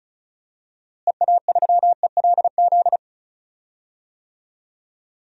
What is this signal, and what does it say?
Morse code, heard as a keyed tone, about 34 wpm, 700 Hz. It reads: EA3ELZ
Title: DXCC-100 Call Signs - 6 Characters - Repeated 2x - 35wpm